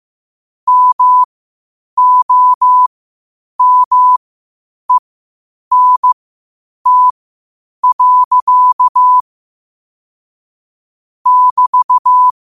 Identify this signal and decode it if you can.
Morse code, heard as a keyed tone, about 15 wpm, spaced as for 8 wpm, 1000 Hz. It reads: MOMENT. <BT>
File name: qotdfiles / 8wpm